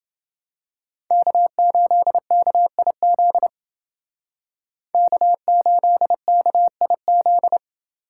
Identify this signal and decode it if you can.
Morse code, heard as a keyed tone, about 28 wpm, 700 Hz. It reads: K8KIZ K8KIZ